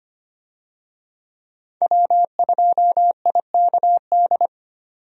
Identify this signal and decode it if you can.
Morse code, heard as a keyed tone, about 25 wpm, 700 Hz. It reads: W2IKD
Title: USA Call Signs - 5 Characters - 25wpm